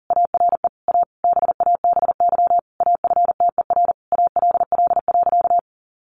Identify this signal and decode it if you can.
Morse code, heard as a keyed tone, about 40 wpm, 700 Hz. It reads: ARE A BABY AFTER ALL.
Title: Sentences from Top 1300 Words 40wpm